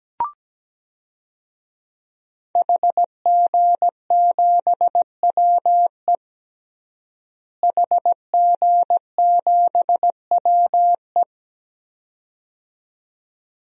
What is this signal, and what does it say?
Morse code, heard as a keyed tone, about 17 wpm, 700 Hz. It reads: HG7WE HG7WE
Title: DXCC-100 Call Signs - 5 Characters - Repeated 2x - 17wpm